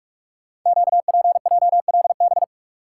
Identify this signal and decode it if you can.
Morse code, heard as a keyed tone, about 44 wpm, 690 Hz. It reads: QPJLB
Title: Code Groups - 5 Letters - 45wpm